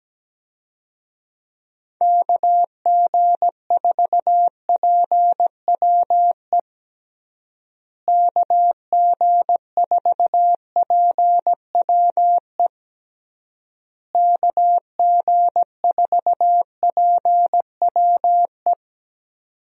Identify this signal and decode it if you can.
Morse code, heard as a keyed tone, about 17 wpm, 700 Hz. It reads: KG4PWE KG4PWE KG4PWE
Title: USA Call Signs - 6 Characters - Repeated 3x - 17wpm